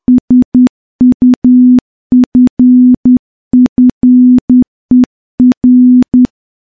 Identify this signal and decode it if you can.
Morse code, heard as a keyed tone, about 10 wpm, 250 Hz. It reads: SUFFER
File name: morse / morse-training-10WPM